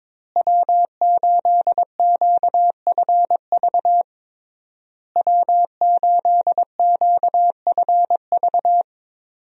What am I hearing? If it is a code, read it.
Morse code, heard as a keyed tone, about 22 wpm, 700 Hz. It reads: W8QFV W8QFV
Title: USA Call Signs - 5 Characters - Repeated 3x - 22wpm